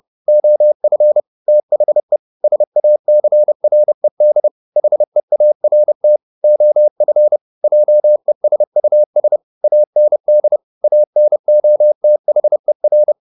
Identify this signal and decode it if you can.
Morse code, heard as a keyed tone, about 30 wpm, 600 Hz. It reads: OF THE SACRED HEART OF JESUS AND ANOTHER
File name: the old man and the sea / the old man and the sea 1 30wpm